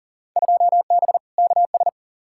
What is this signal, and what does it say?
Morse code, heard as a keyed tone, about 40 wpm, 700 Hz. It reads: 2B KS
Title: Field Day Exchanges Encore - 40wpm